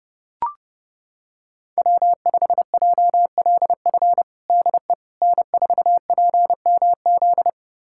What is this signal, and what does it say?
Morse code, heard as a keyed tone, about 32 wpm, 700 Hz. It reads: W5JLF DE N4PMZ
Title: USA Callsign DE Callsign - 5 Characters - 30wpm